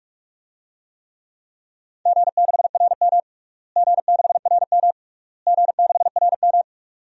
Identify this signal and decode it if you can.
Morse code, heard as a keyed tone, about 44 wpm, 690 Hz. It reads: G6RM G6RM G6RM
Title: DXCC-100 Call Signs - 4 Characters - Repeated 3x - 45wpm